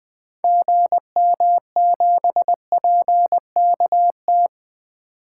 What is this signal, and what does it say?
Morse code, heard as a keyed tone, about 20 wpm, 700 Hz. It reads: GM7PKT
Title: 2023 SOTA Activator Call Signs - 20wpm